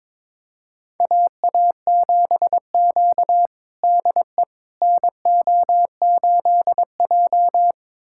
Morse code, heard as a keyed tone, about 22 wpm, 700 Hz. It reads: AA7Q DE NO8J
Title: USA Callsign DE Callsign - 4 Characters - 22wpm